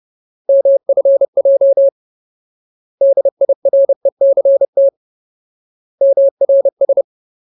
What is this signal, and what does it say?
Morse code, heard as a keyed tone, about 30 wpm, 550 Hz. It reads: MFJ DIRECT MRS
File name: Ultimate Word List / 30WPM Ultimate